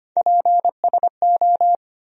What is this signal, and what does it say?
Morse code, heard as a keyed tone, about 25 wpm, 700 Hz. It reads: PSO
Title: USA Call Sign Postfixes - 3 Letters 25wpm